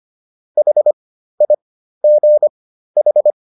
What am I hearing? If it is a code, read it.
Morse code, heard as a keyed tone, about 25 wpm, spaced as for 13 wpm, 600 Hz. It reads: HIGH